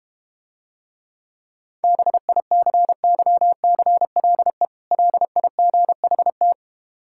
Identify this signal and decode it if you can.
Morse code, heard as a keyed tone, about 32 wpm, 690 Hz. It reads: BICYCLE LIGHT